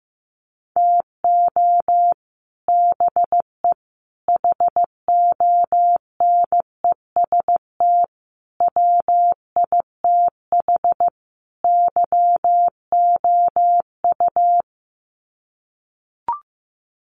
Morse code, heard as a keyed tone, about 14 wpm, 700 Hz. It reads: TO BE HONEST WITH YOU
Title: Sets of 5 Words Encore 15wpm